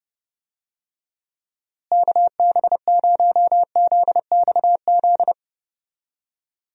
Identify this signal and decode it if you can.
Morse code, heard as a keyed tone, about 30 wpm, 700 Hz. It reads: KB0ZXZ
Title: USA Call Signs - 6 Characters - Repeated 3x - 30wpm